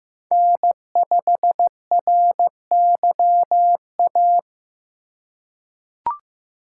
Morse code, heard as a keyed tone, about 15 wpm, 700 Hz. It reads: N5RYA
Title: USA Call Signs - 3x Character Spacing - 15wpm